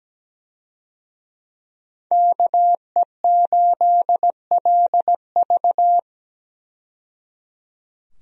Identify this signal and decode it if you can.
Morse code, heard as a keyed tone, about 17 wpm, 700 Hz. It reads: KE8LV
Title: USA Call Signs - 5 Characters - 17wpm